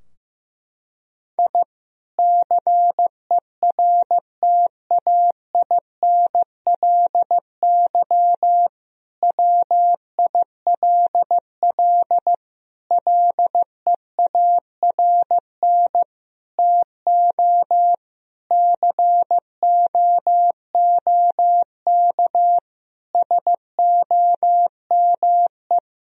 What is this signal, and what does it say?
Morse code, heard as a keyed tone, about 15 wpm, 700 Hz. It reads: I CERTAINLY WILL LEARN TO COOK SOME
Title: Sentences from Top 1000 Words - Review Part 4 - 15wpm